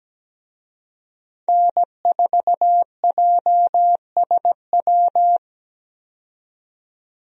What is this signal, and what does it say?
Morse code, heard as a keyed tone, about 17 wpm, 700 Hz. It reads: N4JSW
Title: USA Call Signs - 5 Characters - 17wpm